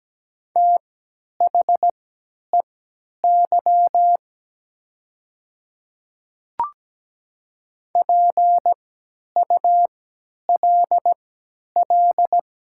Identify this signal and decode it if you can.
Morse code, heard as a keyed tone, about 17 wpm, spaced as for 10 wpm, 700 Hz. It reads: THEY PULL
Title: 3x Character Spacing - Top 500 Words - 17wpm